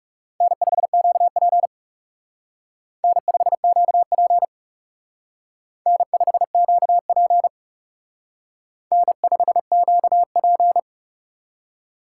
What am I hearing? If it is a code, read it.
Morse code, heard as a keyed tone, about 34 wpm, 700 Hz. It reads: N5QP N5QP N5QP N5QP
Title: Speed Racing - USA 1x2 Call Signs 30wpm